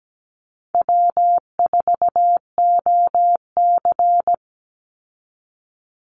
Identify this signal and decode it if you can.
Morse code, heard as a keyed tone, about 17 wpm, 700 Hz. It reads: W4OC
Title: Easy Call Signs Repeated 2x - 17wpm